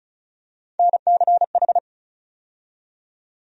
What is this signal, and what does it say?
Morse code, heard as a keyed tone, about 34 wpm, 700 Hz. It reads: NCH
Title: USA Call Sign Postfixes - 3 Letters 35wpm